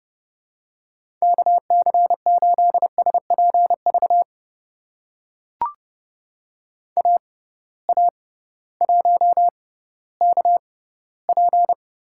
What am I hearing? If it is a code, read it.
Morse code, heard as a keyed tone, about 31 wpm, spaced as for 16 wpm, 700 Hz. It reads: KC8SPV AA1KP
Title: USA Call Signs - 6x Character Spacing - 30wpm